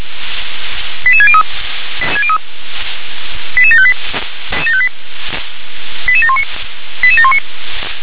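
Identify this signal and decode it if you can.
Five-tone selective call (ZVEI), heard as quick runs of five tones, tones of about 70 ms. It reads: 88603 88758 88719 88719
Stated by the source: ZVEI five-tone selective call 88714 received by SDR.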